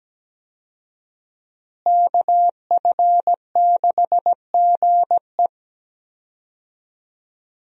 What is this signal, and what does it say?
Morse code, heard as a keyed tone, about 17 wpm, 700 Hz. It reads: KF6GE
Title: USA Call Signs - 5 Characters - Repeated 2x - 17wpm